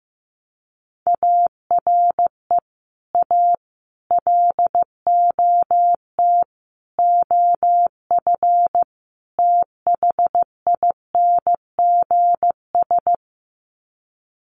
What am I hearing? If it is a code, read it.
Morse code, heard as a keyed tone, about 15 wpm, 700 Hz. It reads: ARE A LOT OF THINGS
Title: Sets of 5 Words Encore2 15wpm